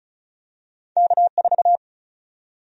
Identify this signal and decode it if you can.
Morse code, heard as a keyed tone, about 34 wpm, 700 Hz. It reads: K4
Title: USA Call Sign Prefixes - Repeated 2x - 35wpm